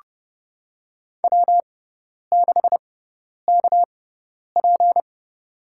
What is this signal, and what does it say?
Morse code, heard as a keyed tone, about 31 wpm, spaced as for 10 wpm, 700 Hz. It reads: W6KP
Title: USA Call Signs - 4 Characters - 6x Character Spacing - 30wpm